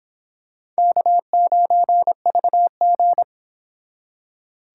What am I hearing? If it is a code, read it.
Morse code, heard as a keyed tone, about 24 wpm, 700 Hz. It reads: K9VG